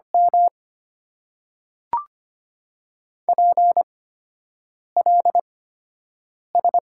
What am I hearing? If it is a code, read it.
Morse code, heard as a keyed tone, about 25 wpm, spaced as for 7 wpm, 700 Hz. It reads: M PLS
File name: Top 90 QSO Elements - 8x Character Spacing - 25wpm